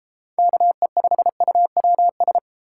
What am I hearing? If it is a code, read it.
Morse code, heard as a keyed tone, about 33 wpm, 700 Hz. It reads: KE5UWS